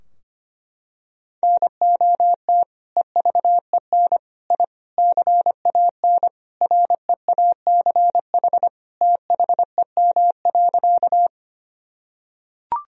Morse code, heard as a keyed tone, about 25 wpm, 700 Hz. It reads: NOT EVEN I CAN REACH THEM.
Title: Sentences from Top 1000 Words - Review Part 2 - 25wpm